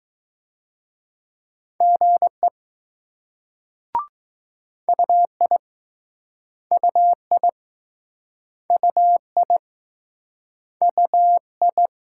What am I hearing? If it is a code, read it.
Morse code, heard as a keyed tone, about 20 wpm, 700 Hz. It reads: GE UI UI UI UI